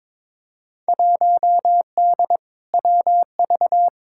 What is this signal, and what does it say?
Morse code, heard as a keyed tone, about 22 wpm, 700 Hz. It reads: 1D WV
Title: Field Day Exchanges - 22wpm